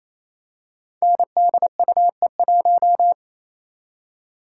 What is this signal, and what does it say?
Morse code, heard as a keyed tone, about 29 wpm, 700 Hz. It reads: NDUE1
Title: Code Groups - 5 Letter:Numbers - 28wpm